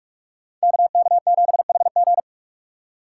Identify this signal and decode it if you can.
Morse code, heard as a keyed tone, about 44 wpm, 700 Hz. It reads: KK7HG